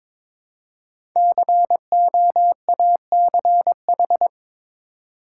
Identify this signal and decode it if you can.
Morse code, heard as a keyed tone, about 22 wpm, 690 Hz. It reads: COACH
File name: 5 Letter Words - First 3 Letters Given - 22wpm